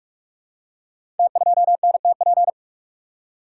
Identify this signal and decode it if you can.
Morse code, heard as a keyed tone, about 44 wpm, 690 Hz. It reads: T2NTP
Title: Code Groups - 5 Letter:Numbers - 45wpm